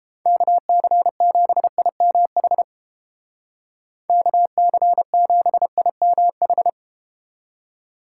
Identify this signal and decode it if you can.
Morse code, heard as a keyed tone, about 31 wpm, 700 Hz. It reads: KC7IMH KC7IMH